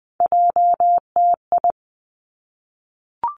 Morse code, heard as a keyed tone, about 20 wpm, 700 Hz. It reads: JTI
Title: Three Challenging Letter-Number Combinations 20wpm